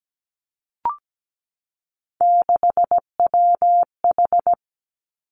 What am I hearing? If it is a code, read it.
Morse code, heard as a keyed tone, about 17 wpm, 700 Hz. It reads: 6WH
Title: Three Challenging Letter-Number Combinations 17wpm